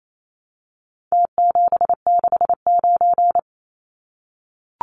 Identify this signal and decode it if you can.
Morse code, heard as a keyed tone, about 28 wpm, 700 Hz. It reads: T769
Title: Numbers - Contest Style - 4-Digit 28wpm